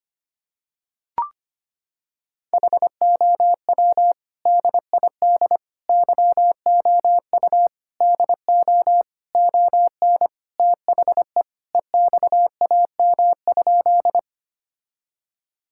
Morse code, heard as a keyed tone, about 25 wpm, 700 Hz. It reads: HOW DID YOU DO ON THE EXAM?
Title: Sentences from Top 1000 Words - Review Part 5 - 25wpm